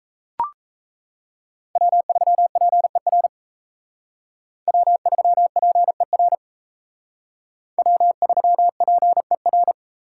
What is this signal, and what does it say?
Morse code, heard as a keyed tone, about 35 wpm, 700 Hz. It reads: W3PER W3PER W3PER
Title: Speed Racing - USA 1x3 Call Signs 28wpm